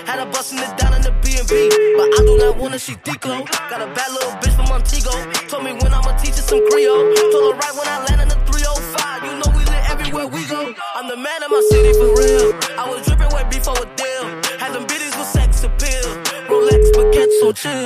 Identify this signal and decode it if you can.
Ringback tone (Europe, 425 Hz), with noise.